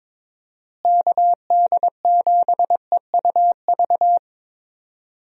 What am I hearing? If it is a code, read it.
Morse code, heard as a keyed tone, about 22 wpm, 700 Hz. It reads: KD7EUV
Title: USA Call Signs - 6 Characters - Repeated 3x - 22wpm